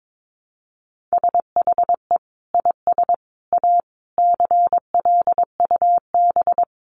Morse code, heard as a keyed tone, about 22 wpm, 700 Hz. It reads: SHE IS A CLUB
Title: Sentences from Top 1200 Words 22wpm